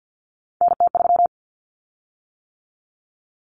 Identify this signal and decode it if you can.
Morse code, heard as a keyed tone, about 50 wpm, 700 Hz. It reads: NT3
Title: Common Cut Numbers - 3-Digit 50wpm